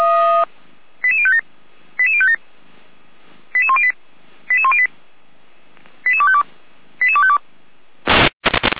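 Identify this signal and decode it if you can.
Five-tone selective call (ZVEI), heard as quick runs of five tones, tones of about 70 ms. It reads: 88057 88057 88108 88108 88252 88252
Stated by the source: ZVEI five-tone selective call received by SDR.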